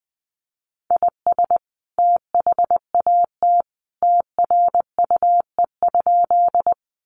Morse code, heard as a keyed tone, about 20 wpm, 700 Hz. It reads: IS THAT TRUE?
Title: Sentences from Top 400 Words 20wpm